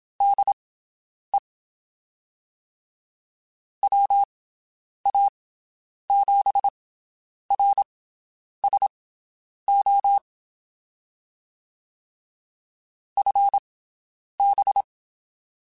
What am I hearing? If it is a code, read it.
Morse code, heard as a keyed tone, about 27 wpm, spaced as for 9 wpm, 800 Hz. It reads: DE WA7RSO FB
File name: (spaced) / QSOs Spaced-mp3s / WD0BGP 1 @25 sp